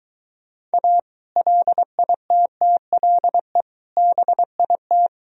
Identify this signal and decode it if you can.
Morse code, heard as a keyed tone, about 23 wpm, 700 Hz. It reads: A LITTLE BIT